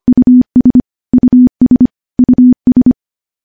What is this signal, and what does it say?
Morse code, heard as a keyed tone, about 25 wpm, 250 Hz. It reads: US US US
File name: morse-training-25WPM